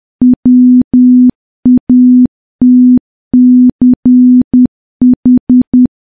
Morse code, heard as a keyed tone, about 10 wpm, 250 Hz. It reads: WATCH
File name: morse / morse-training-10WPM